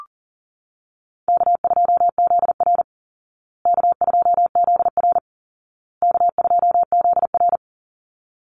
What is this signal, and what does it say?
Morse code, heard as a keyed tone, about 40 wpm, 700 Hz. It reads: K2ZR K2ZR K2ZR
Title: Call Signs Repeated 2x - 40wpm